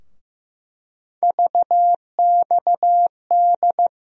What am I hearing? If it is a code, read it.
Morse code, heard as a keyed tone, about 15 wpm, 700 Hz. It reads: VXD
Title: USA Call Sign Postfixes - 3 Letters 15wpm